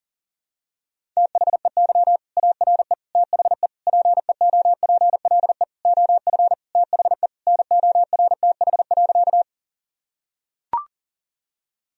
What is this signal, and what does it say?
Morse code, heard as a keyed tone, about 37 wpm, 700 Hz. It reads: THEY ARE THE PEOPLE OF THE NORTH.
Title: Sentences from Top 1000 Words - Review Part 1 - 40wpm